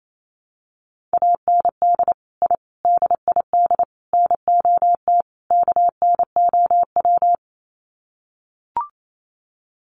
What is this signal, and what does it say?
Morse code, heard as a keyed tone, about 28 wpm, 700 Hz. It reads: AND I DID NOT KNOW
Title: Sets of 5 Words Encore 28wpm